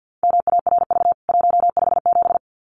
Morse code, heard as a keyed tone, about 49 wpm, 700 Hz. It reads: MARV 157
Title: CWT Exchange 50wpm